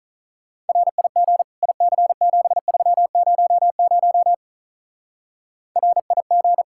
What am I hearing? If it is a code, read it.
Morse code, heard as a keyed tone, about 43 wpm, 700 Hz. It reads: RIG IC7300 RIG
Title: Speed-Racing Common QSO Phrases - 35wpm